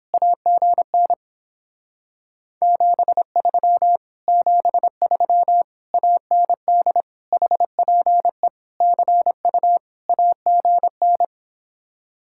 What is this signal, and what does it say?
Morse code, heard as a keyed tone, about 29 wpm, 700 Hz. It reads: AGN 73 73 AND HPE CU AGN